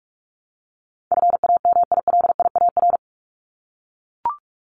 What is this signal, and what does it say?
Morse code, heard as a keyed tone, about 44 wpm, 700 Hz. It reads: FAMILIAR
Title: Top 751-1000 Words 45wpm